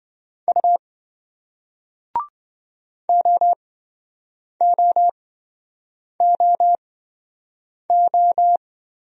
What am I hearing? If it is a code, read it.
Morse code, heard as a keyed tone, about 28 wpm, spaced as for 7 wpm, 700 Hz. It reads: U OOOO